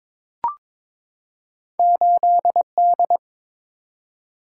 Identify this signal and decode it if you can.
Morse code, heard as a keyed tone, about 22 wpm, 700 Hz. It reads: 8D
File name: practice / 22wpm / Two Letter-Number Combinations - 22wpm